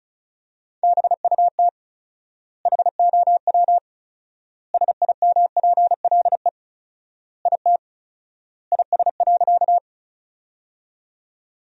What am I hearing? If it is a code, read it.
Morse code, heard as a keyed tone, about 34 wpm, 700 Hz. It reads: BUT HOW SIMPLE IT IS.
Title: 4x Word-Spacing - Sentences from Top 400 Words - 35wpm